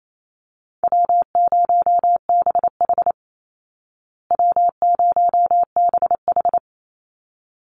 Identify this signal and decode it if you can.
Morse code, heard as a keyed tone, about 28 wpm, 700 Hz. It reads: W0BH W0BH
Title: Easy Call Signs Repeated 2x - 28wpm